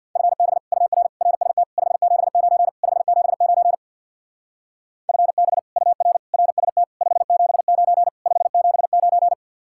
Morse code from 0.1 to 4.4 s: FB UR RST 579 579

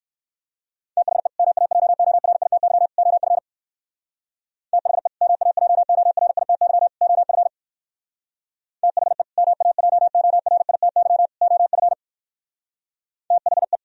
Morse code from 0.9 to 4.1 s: THE MAJORITY OF

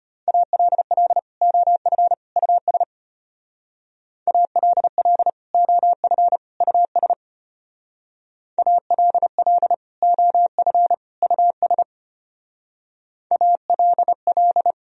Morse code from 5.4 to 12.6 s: OF US ALL OF US

